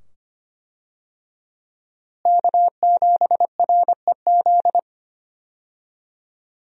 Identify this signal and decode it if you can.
Morse code, heard as a keyed tone, about 25 wpm, 700 Hz. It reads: K7REZ
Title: USA Call Signs - 5 Characters - Repeated 2x - 25wpm